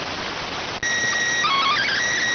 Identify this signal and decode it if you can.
SSTV (slow-scan television), PD 290.